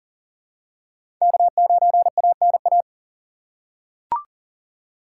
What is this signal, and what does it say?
Morse code, heard as a keyed tone, about 40 wpm, 700 Hz. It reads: K9ANA
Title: USA Call Signs - 5 Characters - 6x Character Spacing - 40wpm